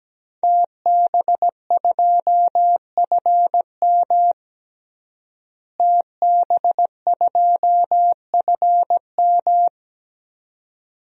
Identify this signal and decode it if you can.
Morse code, heard as a keyed tone, about 17 wpm, 700 Hz. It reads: TB2FM TB2FM
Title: DXCC-100 Call Signs - 5 Characters - Repeated 2x - 17wpm